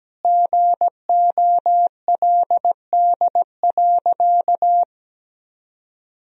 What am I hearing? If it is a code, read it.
Morse code, heard as a keyed tone, about 17 wpm, 700 Hz. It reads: GOLD.
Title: Sentences from Top 1000 Words - Review Part 3 - 17wpm